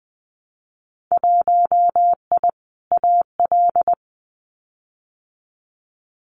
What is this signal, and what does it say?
Morse code, heard as a keyed tone, about 20 wpm, 700 Hz. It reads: 1I AL
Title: Winter Field Day Exchanges 20wpm